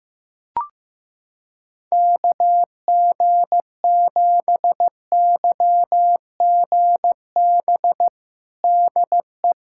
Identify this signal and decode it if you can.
Morse code, heard as a keyed tone, about 15 wpm, 700 Hz. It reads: KG7YGB DE
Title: USA Callsign DE Callsign 15wpm